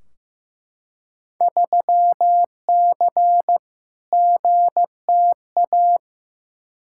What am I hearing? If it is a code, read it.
Morse code, heard as a keyed tone, about 15 wpm, 700 Hz. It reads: 3C GTA
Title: Field Day Exchanges Encore - 15wpm